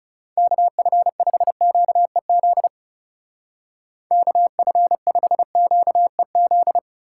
Morse code, heard as a keyed tone, about 37 wpm, 700 Hz. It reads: KF5QEZ KF5QEZ
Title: Speed Racing - USA 2x3 Call Signs 30wpm